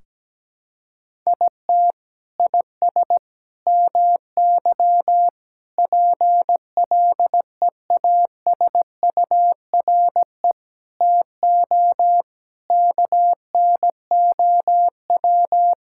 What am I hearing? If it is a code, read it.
Morse code, heard as a keyed tone, about 17 wpm, 700 Hz. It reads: IT IS MY PLEASURE TO KNOW
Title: Sentences from Top 1600 Words 17wpm